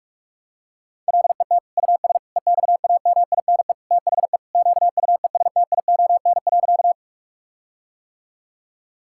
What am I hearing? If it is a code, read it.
Morse code, heard as a keyed tone, about 41 wpm, 700 Hz. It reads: LET US EXAMINE THE QUESTION.
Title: Sentences from Top 1600 Words 45wpm